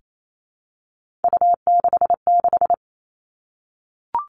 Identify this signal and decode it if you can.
Morse code, heard as a keyed tone, about 28 wpm, 700 Hz. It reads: U66
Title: Three Challenging Letter-Number Combinations 28wpm